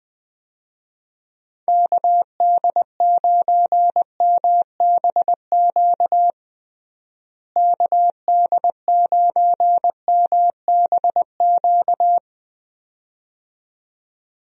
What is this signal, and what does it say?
Morse code, heard as a keyed tone, about 20 wpm, 700 Hz. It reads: KD9MBQ KD9MBQ
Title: USA Call Signs - 6 Characters - Repeated 2x - 20wpm